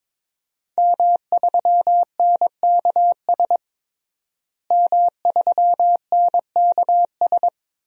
Morse code, heard as a keyed tone, about 22 wpm, 700 Hz. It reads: M3NKS M3NKS
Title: DXCC-100 Call Signs - 5 Characters - Repeated 3x - 22wpm